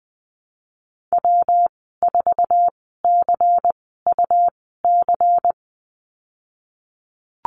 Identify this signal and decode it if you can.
Morse code, heard as a keyed tone, about 20 wpm, spaced as for 15 wpm, 700 Hz. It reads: W4CUC
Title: 128 Lesson 21g - Number 6 - Call Signs 20wpm